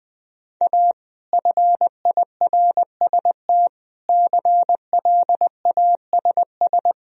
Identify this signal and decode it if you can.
Morse code, heard as a keyed tone, about 21 wpm, 700 Hz. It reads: A FIRST CLASS
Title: Sentences from Top 1000 Words - Review Part 2 - 20wpm